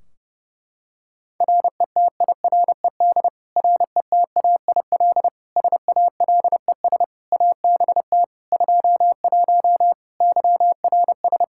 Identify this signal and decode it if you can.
Morse code, heard as a keyed tone, about 30 wpm, 700 Hz. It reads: RETIRED RETAIL SALES ABT 21 YRS
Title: Practice QSO - Exchange 3 - More Info - 30wpm